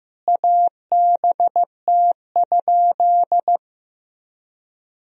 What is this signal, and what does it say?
Morse code, heard as a keyed tone, about 15 wpm, 700 Hz. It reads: ABT?
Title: Sentences from Top 100 Words and Abbreviations 15wpm